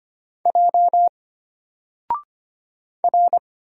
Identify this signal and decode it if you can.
Morse code, heard as a keyed tone, about 25 wpm, 700 Hz. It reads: J R
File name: Single-Letter-Number-25wpm